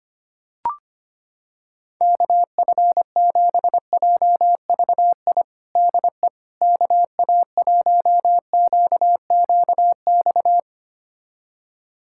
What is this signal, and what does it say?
Morse code, heard as a keyed tone, about 25 wpm, 700 Hz. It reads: KF7JVI DE KA1QQX
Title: USA Callsign DE Callsign - 6 Characters - 25wpm